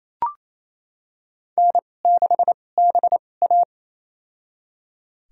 Morse code, heard as a keyed tone, about 28 wpm, spaced as for 20 wpm, 700 Hz. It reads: N6BA
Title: USA Call Signs - 4 Characters - 2x Character Spacing - 28wpm